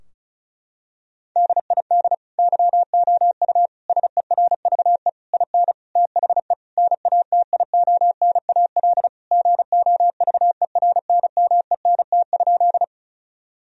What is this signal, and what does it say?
Morse code, heard as a keyed tone, about 34 wpm, 700 Hz. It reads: DID YOU SERVE IN THE NATIONAL GOVERNMENT?
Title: Sentences from Top 1000 Words - Review Part 4 - 35wpm